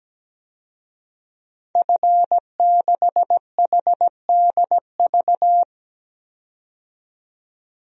Morse code, heard as a keyed tone, about 17 wpm, 700 Hz. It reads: F6HDV
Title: DXCC-100 Call Signs - 5 Characters - 17wpm